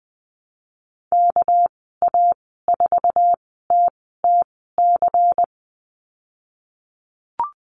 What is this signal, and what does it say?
Morse code, heard as a keyed tone, about 20 wpm, spaced as for 15 wpm, 700 Hz. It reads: KA4TTC
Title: 200 Lesson 32g - Letter k - Call Signs 20wpm